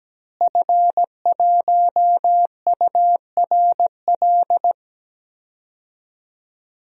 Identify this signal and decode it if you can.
Morse code, heard as a keyed tone, about 17 wpm, 700 Hz. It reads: F1URL
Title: DXCC-100 Call Signs - 5 Characters - Repeated 3x - 17wpm